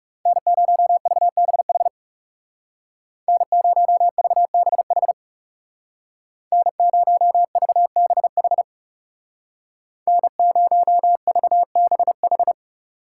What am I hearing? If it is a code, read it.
Morse code, heard as a keyed tone, about 35 wpm, 700 Hz. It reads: N0VBH N0VBH N0VBH N0VBH
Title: Speed Racing - USA 1x3 Call Signs 30wpm